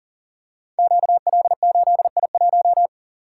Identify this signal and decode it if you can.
Morse code, heard as a keyed tone, about 39 wpm, 690 Hz. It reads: QL8I1
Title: Code Groups - 5 Letter:Numbers - 40wpm